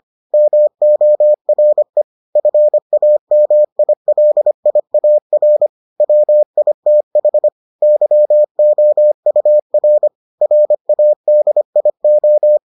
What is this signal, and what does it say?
Morse code, heard as a keyed tone, about 25 wpm, 600 Hz. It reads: MORE FAMILIAR WITH YOUR RADIO